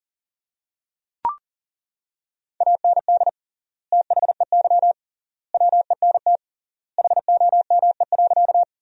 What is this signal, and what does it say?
Morse code, heard as a keyed tone, about 40 wpm, 700 Hz. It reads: AND THEY WENT HOME.
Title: 3x Word-Spacing - Sentences from Top 200 Words - 40wpm